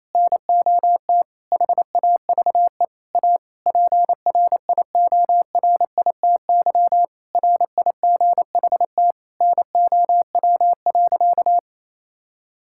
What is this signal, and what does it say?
Morse code, heard as a keyed tone, about 28 wpm, 700 Hz. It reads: NOT HAVE A PRIORITY RIGHT NOW.
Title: Sentences from Top 1600 Words 28wpm